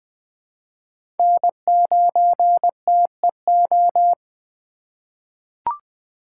Morse code, heard as a keyed tone, about 20 wpm, 700 Hz. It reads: N9TEO
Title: USA Call Signs - 5 Characters - 3x Character Spacing - 20wpm